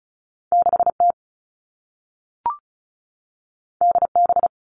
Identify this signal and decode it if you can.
Morse code, heard as a keyed tone, about 36 wpm, 690 Hz. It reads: 6T DB
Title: All Cut Numbers - 2-Digit 35wpm